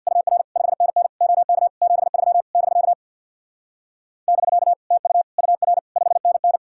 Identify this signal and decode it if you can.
Morse code, heard as a keyed tone, about 65 wpm, 700 Hz. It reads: UR 5NN OK 73 <BK> <BK> TU UR 5NN